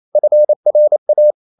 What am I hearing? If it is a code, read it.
Morse code, heard as a keyed tone, about 28 wpm, 600 Hz. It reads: FRA
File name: Ferie QRQ 28wpm